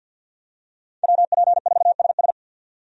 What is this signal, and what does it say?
Morse code, heard as a keyed tone, about 52 wpm, 700 Hz. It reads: WP4SS
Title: USA Call Signs - 5 Characters - 50wpm